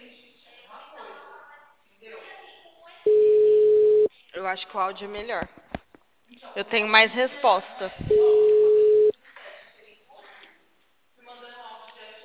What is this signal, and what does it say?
Ringback tone (Europe, 425 Hz), with speech.